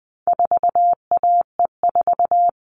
Morse code, heard as a keyed tone, about 20 wpm, 700 Hz. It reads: 4AE4
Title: All Cut Numbers - 4-Digit 20wpm